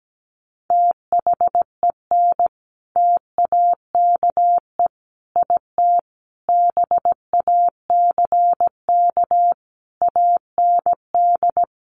Morse code, heard as a keyed tone, about 17 wpm, 700 Hz. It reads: THEN TAKE IT BACK AND
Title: Sentences from Top 1500 Words 17wpm